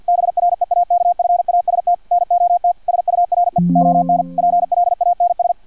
Morse code, heard as a keyed tone, about 50 wpm, 700 Hz. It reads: BREAKFAST NOT SURE ON PLANS
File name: archived / CRSnet